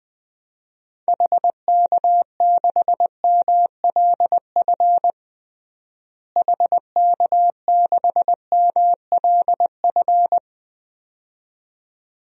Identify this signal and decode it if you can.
Morse code, heard as a keyed tone, about 20 wpm, 700 Hz. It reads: HK6MLF HK6MLF
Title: DXCC-100 Call Signs - 6 Characters - Repeated 2x - 20wpm